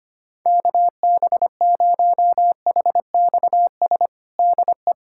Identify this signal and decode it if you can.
Morse code, heard as a keyed tone, about 25 wpm, 700 Hz. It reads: KB0HXS DE
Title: USA Callsign DE Callsign - 6 Characters - 25wpm